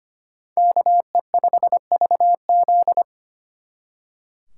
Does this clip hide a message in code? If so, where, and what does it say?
Yes: Morse code from 0.6 to 3.0 s, heard as a keyed tone, about 33 wpm, 700 Hz. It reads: KE5VZ